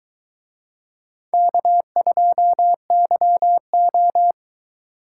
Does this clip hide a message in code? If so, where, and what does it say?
Yes: Morse code from 1.3 to 4.3 s, heard as a keyed tone, about 22 wpm, 700 Hz. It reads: K2YO